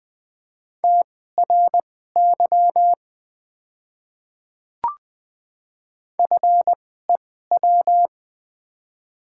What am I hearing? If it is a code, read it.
Morse code, heard as a keyed tone, about 20 wpm, spaced as for 15 wpm, 700 Hz. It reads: TRY FEW
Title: 2x Character Spacing - Top 500 Words - 20wpm